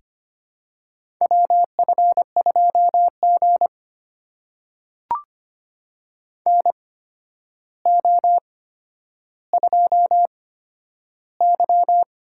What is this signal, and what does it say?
Morse code, heard as a keyed tone, about 25 wpm, spaced as for 11 wpm, 700 Hz. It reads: WF2G NO2Y
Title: USA Call Signs - 4 Characters - 8x Character Spacing - 25wpm